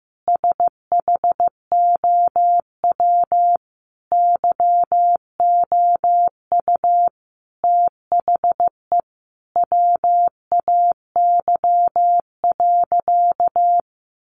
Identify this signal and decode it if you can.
Morse code, heard as a keyed tone, about 15 wpm, 700 Hz. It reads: SHOW YOU THE WAY.
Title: Sentences from Top 300 Words 15wpm